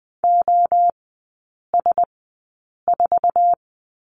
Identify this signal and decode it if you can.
Morse code, heard as a keyed tone, about 20 wpm, spaced as for 8 wpm, 700 Hz. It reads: OS4
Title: 047 Lesson 9e - Number 4 - Three Characters 20wpm